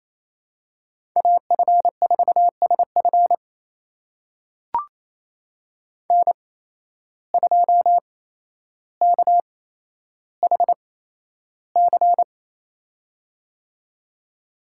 Morse code, heard as a keyed tone, about 28 wpm, spaced as for 12 wpm, 700 Hz. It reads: AF4SF N2KHC
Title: USA Call Signs - 5 Characters - 8x Character Spacing - 28wpm